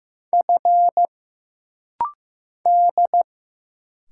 Morse code, heard as a keyed tone, about 15 wpm, 700 Hz. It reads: F D
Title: Single Letter-Number Alt - Rapid-Fire 15wpm